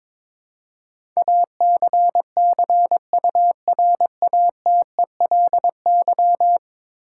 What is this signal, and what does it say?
Morse code, heard as a keyed tone, about 22 wpm, 700 Hz. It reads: ACCURATELY
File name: practice / 22wpm / 10 Letter Words - First 3 Letters Given - 22wpm